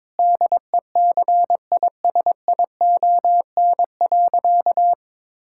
Morse code, heard as a keyed tone, about 22 wpm, 700 Hz. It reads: DECISION.
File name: practice / 22wpm / Sentences with 8-Word Phrases - 22wpm